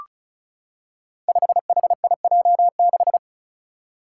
Morse code, heard as a keyed tone, about 35 wpm, 700 Hz. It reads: 5HIJ6
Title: Five Challenging Letter-Number Combinations 35wpm